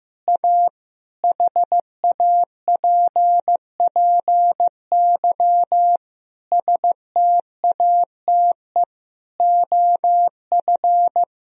Morse code, heard as a keyed tone, about 16 wpm, 700 Hz. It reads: A HAPPY STATE OF
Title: Sentences from Top 1000 Words - Review Part 2 - 15wpm